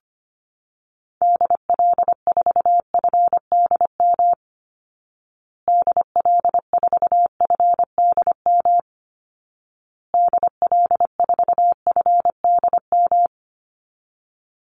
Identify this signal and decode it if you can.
Morse code, heard as a keyed tone, about 25 wpm, 700 Hz. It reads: DL4FDM DL4FDM DL4FDM
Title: Call Signs Repeated 2x - 25wpm